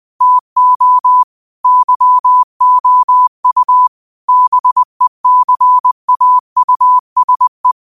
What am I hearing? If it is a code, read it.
Morse code, heard as a keyed tone, about 20 wpm, 1000 Hz. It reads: TO YOU BECAUSE